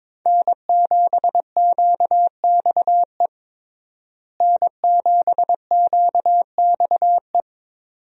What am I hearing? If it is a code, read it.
Morse code, heard as a keyed tone, about 21 wpm, 700 Hz. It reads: N7QXE N7QXE
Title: USA Call Signs - 5 Characters - Repeated 2x - 22wpm